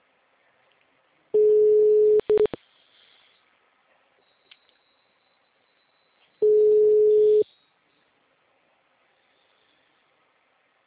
Ringback tone (Europe, 425 Hz).